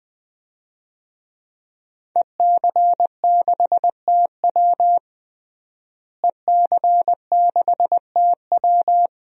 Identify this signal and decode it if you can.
Morse code, heard as a keyed tone, about 20 wpm, 700 Hz. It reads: EC6TW EC6TW
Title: DXCC-100 Call Signs - 5 Characters - Repeated 2x - 20wpm